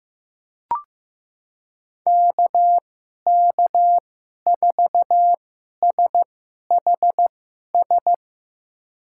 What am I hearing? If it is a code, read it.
Morse code, heard as a keyed tone, about 15 wpm, spaced as for 11 wpm, 700 Hz. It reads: KK4SHS
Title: USA Call Signs - 6 Characters - 2x Character Spacing - 15wpm